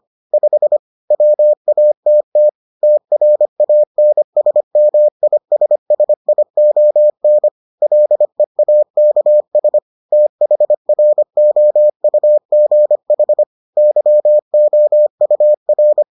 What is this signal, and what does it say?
Morse code, heard as a keyed tone, about 25 wpm, 600 Hz. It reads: 5 WATT TRANSMISSION LEAKS THROUGH YOUR